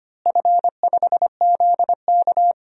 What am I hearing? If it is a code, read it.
Morse code, heard as a keyed tone, about 25 wpm, 700 Hz. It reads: F5ZK
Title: DXCC-100 Call Signs - 4 Characters - Repeated 2x - 25wpm